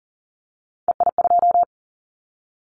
Morse code, heard as a keyed tone, about 40 wpm, 700 Hz. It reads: EI2
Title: Callsign Prefixes for Top 50 DXCC - 40wpm